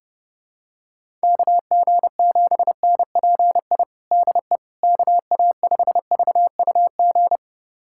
Morse code, heard as a keyed tone, about 30 wpm, 700 Hz. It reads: KG7NPI DE KA5VUG
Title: USA Callsign DE Callsign - 6 Characters - 30wpm